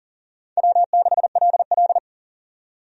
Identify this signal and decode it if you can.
Morse code, heard as a keyed tone, about 40 wpm, 700 Hz. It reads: W6LL
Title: USA Call Signs - 4 Characters - 40wpm